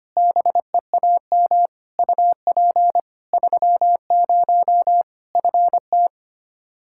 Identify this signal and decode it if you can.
Morse code, heard as a keyed tone, about 25 wpm, 700 Hz. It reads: BEAM UP 30 FT
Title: Practice QSO - Exchange 2 - Info - 25wpm